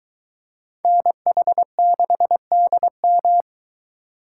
Morse code, heard as a keyed tone, about 22 wpm, 700 Hz. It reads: NH6DM